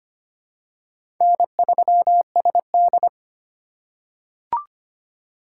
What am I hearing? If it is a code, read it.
Morse code, heard as a keyed tone, about 25 wpm, 700 Hz. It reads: N3SD
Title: USA Call Signs - 4 Characters - 4x Character Spacing - 25wpm